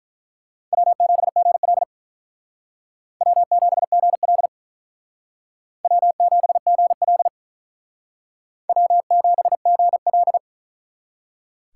Morse code, heard as a keyed tone, about 43 wpm, 700 Hz. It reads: W7GL W7GL W7GL W7GL
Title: Speed Racing - USA 1x2 Call Signs 35wpm